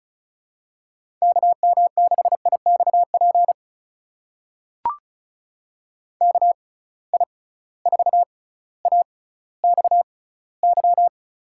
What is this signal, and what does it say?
Morse code, heard as a keyed tone, about 34 wpm, spaced as for 18 wpm, 700 Hz. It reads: KM6IXP KI4AXY
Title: USA Call Signs - 6 Characters - 6x Character Spacing - 35wpm